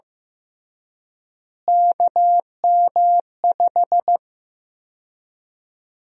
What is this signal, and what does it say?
Morse code, heard as a keyed tone, about 15 wpm, 700 Hz. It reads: KM5
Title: USA Call Sign Prefixes - Repeated 2x - 15wpm